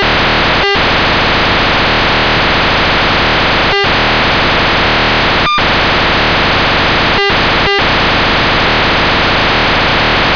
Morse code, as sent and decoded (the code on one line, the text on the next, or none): none